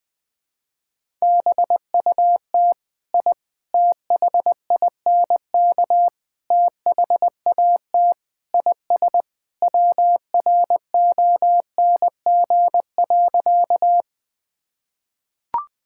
Morse code, heard as a keyed tone, about 21 wpm, 700 Hz. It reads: BUT I THINK THAT IS WRONG.
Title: Sentences with 4-Word Phrases - 20wpm